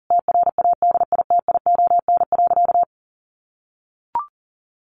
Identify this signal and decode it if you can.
Morse code, heard as a keyed tone, about 40 wpm, 700 Hz. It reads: TRADITION.
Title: Sentences from Top 1500 Words 40wpm